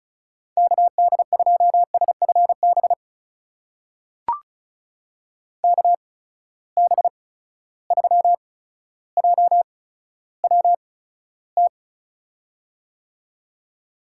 Morse code, heard as a keyed tone, about 36 wpm, spaced as for 15 wpm, 700 Hz. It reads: KD2SFB KB3JWT
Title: USA Call Signs - 6 Characters - 8x Character Spacing - 35wpm